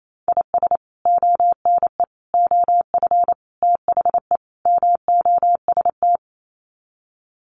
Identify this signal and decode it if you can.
Morse code, heard as a keyed tone, about 28 wpm, 700 Hz. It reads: IS ONE OF THE MOST
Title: Sets of 5 Words 28wpm